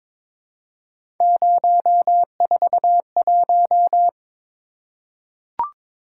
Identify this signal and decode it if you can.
Morse code, heard as a keyed tone, about 23 wpm, 700 Hz. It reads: 041
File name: practice / 22wpm / Three Letter-Number Combinations - 22wpm